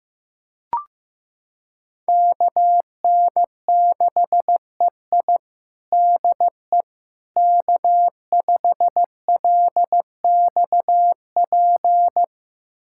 Morse code, heard as a keyed tone, about 16 wpm, 700 Hz. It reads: KN6EI DE K5LXP
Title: USA Callsign DE Callsign - 5 Characters - 15wpm